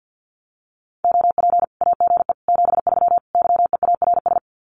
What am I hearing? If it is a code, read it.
Morse code, heard as a keyed tone, about 50 wpm, 700 Hz. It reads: OP AGE 73 YEARS
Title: Common QSO Phrases 50wpm